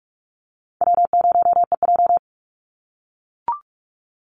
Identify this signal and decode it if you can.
Morse code, heard as a keyed tone, about 44 wpm, 700 Hz. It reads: W0EJ
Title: Easy Call Signs - 45wpm